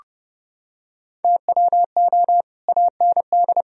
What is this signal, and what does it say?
Morse code, heard as a keyed tone, about 30 wpm, 700 Hz. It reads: TWO AND